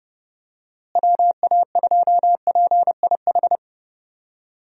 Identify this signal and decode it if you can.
Morse code, heard as a keyed tone, about 30 wpm, 700 Hz. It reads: WA2PIH
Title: USA Call Signs - 6 Characters - Repeated 3x - 30wpm